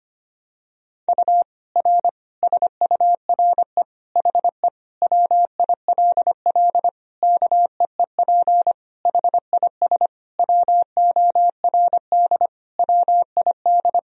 Morse code, heard as a keyed tone, about 25 wpm, 700 Hz. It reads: U R SURE HE WILL KEEP HIS WORD WID
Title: Sentences from Top 300 Words and Abbreviations 25wpm